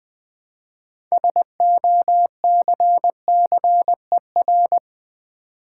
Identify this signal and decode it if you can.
Morse code, heard as a keyed tone, about 20 wpm, 700 Hz. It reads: SOCCER
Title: Build A Word - 2 Letters At A Time - 6-Letter Words - 20wpm